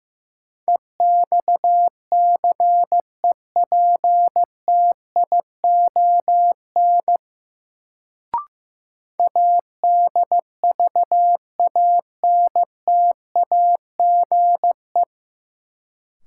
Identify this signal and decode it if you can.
Morse code, heard as a keyed tone, about 15 wpm, 700 Hz. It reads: EXCEPTION ADVANTAGE
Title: Sets of 4 Words - New Word Drill - 15wpm